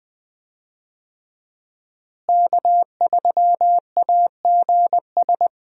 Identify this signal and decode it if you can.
Morse code, heard as a keyed tone, about 20 wpm, 700 Hz. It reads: K3AGS
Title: USA Call Signs - 5 Characters - Repeated 3x - 20wpm